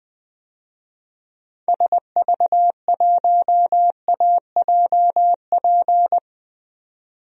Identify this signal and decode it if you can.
Morse code, heard as a keyed tone, about 20 wpm, 700 Hz. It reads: SV1AJP
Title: DXCC-100 Call Signs - 6 Characters - Repeated 3x - 20wpm